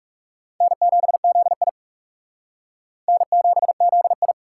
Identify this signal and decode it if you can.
Morse code, heard as a keyed tone, about 37 wpm, 700 Hz. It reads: N7ZI N7ZI
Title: Speed Racing - USA 1x2 Call Signs 30wpm